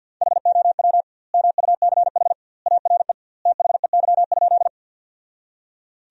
Morse code, heard as a keyed tone, about 52 wpm, 700 Hz. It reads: HOW MUCH ARE THEY?
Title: 2x Word-Spacing - Sentences from Top 200 Words - 50wpm